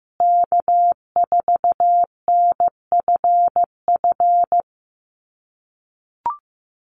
Morse code, heard as a keyed tone, about 15 wpm, 700 Hz. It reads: K4NFF
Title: USA Call Signs Repeated 2x 15wpm